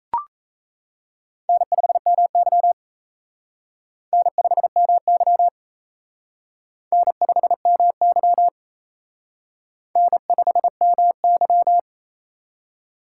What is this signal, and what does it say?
Morse code, heard as a keyed tone, about 36 wpm, 700 Hz. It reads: N5MY N5MY N5MY N5MY